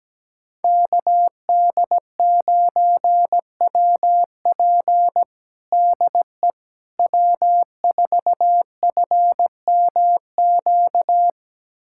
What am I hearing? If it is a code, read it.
Morse code, heard as a keyed tone, about 17 wpm, 700 Hz. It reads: KD9WP DE W4FMQ
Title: USA Callsign DE Callsign - 5 Characters - 17wpm